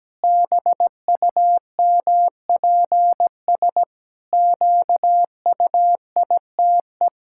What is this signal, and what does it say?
Morse code, heard as a keyed tone, about 17 wpm, 700 Hz. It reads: BUMPS QUITE
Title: Pangram Challenge v2 - 17wpm